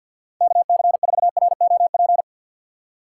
Morse code, heard as a keyed tone, about 51 wpm, 700 Hz. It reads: KC4ROP
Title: USA Call Signs - 6 Characters - 6x Character Spacing - 50wpm